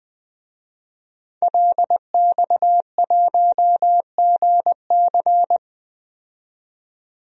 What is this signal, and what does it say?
Morse code, heard as a keyed tone, about 20 wpm, 700 Hz. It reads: LX1GC
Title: DXCC-100 Call Signs - 5 Characters - 20wpm